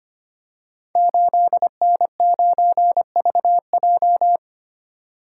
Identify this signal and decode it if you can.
Morse code, heard as a keyed tone, about 25 wpm, 700 Hz. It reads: 8N9VJ
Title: Code Groups - 5 Letter:Numbers - 25wpm